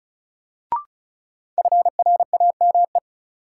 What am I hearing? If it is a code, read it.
Morse code, heard as a keyed tone, about 36 wpm, 700 Hz. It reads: FRAME
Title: Sentences from Top 1400 Words - New Word Drill Alt - 35wpm